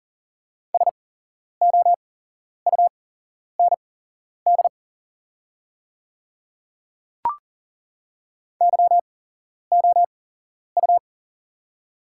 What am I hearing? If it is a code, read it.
Morse code, heard as a keyed tone, about 40 wpm, spaced as for 11 wpm, 690 Hz. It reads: SOUND YOU
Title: 8x Character Spacing - Top 100 Words - 40wpm